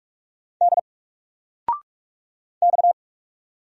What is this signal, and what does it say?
Morse code, heard as a keyed tone, about 44 wpm, 700 Hz. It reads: D X
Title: Single Letter-Number 45wpm